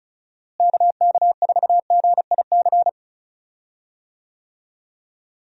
Morse code, heard as a keyed tone, about 36 wpm, 700 Hz. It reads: KK4GIC
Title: USA Call Signs - 6 Characters - Repeated 2x - 35wpm